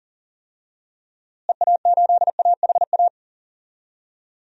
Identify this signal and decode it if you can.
Morse code, heard as a keyed tone, about 37 wpm, 700 Hz. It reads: EA8AHA
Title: DXCC-100 Call Signs - 6 Characters - Repeated 2x - 40wpm